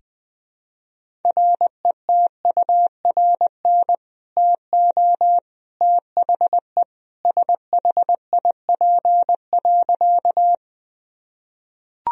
Morse code, heard as a keyed tone, about 21 wpm, 700 Hz. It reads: RETURN TO THE SHIP.
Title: Sentences from Top 1000 Words - Review Part 2 - 20wpm